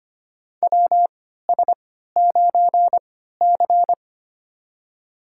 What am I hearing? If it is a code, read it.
Morse code, heard as a keyed tone, about 25 wpm, spaced as for 14 wpm, 700 Hz. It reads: WS9C
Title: USA Call Signs - 4 Characters - 3x Character Spacing - 25wpm